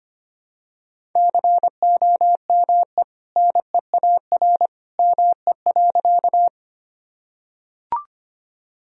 Morse code, heard as a keyed tone, about 25 wpm, 700 Hz. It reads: COME NEAR ME.
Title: Sentences from Top 1000 Words - Review Part 1 - 25wpm